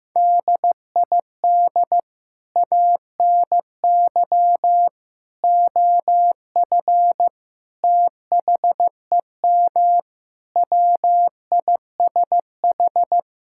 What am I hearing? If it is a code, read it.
Morse code, heard as a keyed tone, about 15 wpm, 700 Hz. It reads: DID ANY OF THEM WISH
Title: Sentences from Top 1600 Words 15wpm